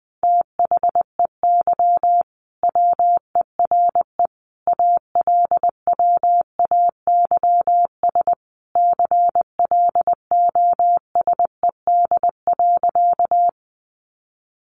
Morse code, heard as a keyed tone, about 20 wpm, 700 Hz. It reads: THEY WERE ALWAYS CLOSED.
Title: Sentences from Top 1400 Words 20wpm